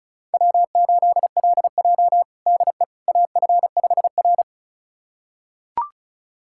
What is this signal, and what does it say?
Morse code, heard as a keyed tone, about 36 wpm, 700 Hz. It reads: W8LJ DE AF5R
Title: USA Callsign DE Callsign - 4 Characters - 35wpm